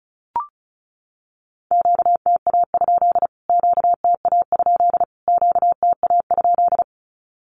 Morse code, heard as a keyed tone, about 34 wpm, 700 Hz. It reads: QTA? QTA? QTA?
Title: Q-Codes Repeated 3x 35wpm